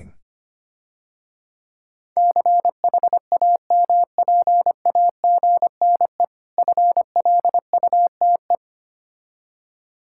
Morse code, heard as a keyed tone, about 26 wpm, 700 Hz. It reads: CHAMPAGNE FLUTE
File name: practice / 25wpm / Difficult Words - Speed-Racing with Hints - 25wpm